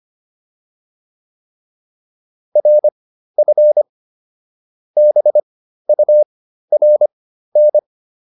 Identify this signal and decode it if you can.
Morse code, heard as a keyed tone, about 25 wpm, spaced as for 13 wpm, 600 Hz. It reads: RF BURN